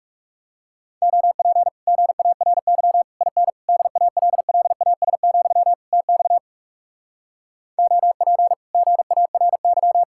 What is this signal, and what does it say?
Morse code, heard as a keyed tone, about 38 wpm, 700 Hz. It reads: OP GARY IN DALLAS, TX OP GARY
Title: Speed-Racing Common QSO Phrases - 30wpm